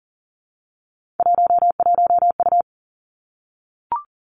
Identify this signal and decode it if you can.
Morse code, heard as a keyed tone, about 38 wpm, 700 Hz. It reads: 11U
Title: Three Challenging Letter-Number Combinations 40wpm